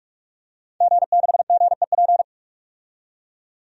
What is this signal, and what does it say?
Morse code, heard as a keyed tone, about 46 wpm, 700 Hz. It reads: G6GEP
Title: DXCC-100 Call Signs - 5 Characters - 45wpm